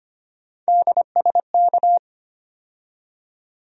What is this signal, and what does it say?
Morse code, heard as a keyed tone, about 25 wpm, 700 Hz. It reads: DSK
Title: USA Call Sign Postfixes - 3 Letters 25wpm